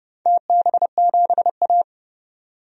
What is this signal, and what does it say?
Morse code, heard as a keyed tone, about 30 wpm, 700 Hz. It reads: TB7A
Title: DXCC-100 Call Signs - 4 Characters - 30wpm